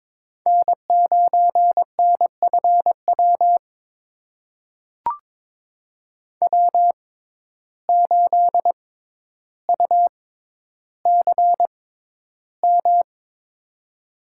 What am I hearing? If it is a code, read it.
Morse code, heard as a keyed tone, about 22 wpm, spaced as for 11 wpm, 700 Hz. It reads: N9NFW W8UCM
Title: USA Call Signs - 6x Character Spacing - 22wpm